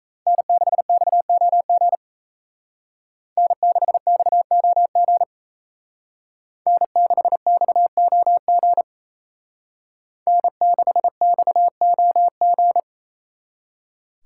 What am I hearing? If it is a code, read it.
Morse code, heard as a keyed tone, about 36 wpm, 700 Hz. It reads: N6XOG N6XOG N6XOG N6XOG